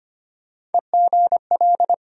Morse code, heard as a keyed tone, about 25 wpm, 700 Hz. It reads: EGL
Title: USA Call Sign Postfixes - 3 Letters 25wpm